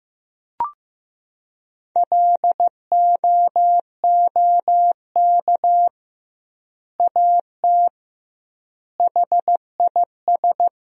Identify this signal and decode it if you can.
Morse code, heard as a keyed tone, about 15 wpm, 700 Hz. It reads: LOOK AT HIS
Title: 2x Word-Spacing - Sentences from Top 300 Words - 15wpm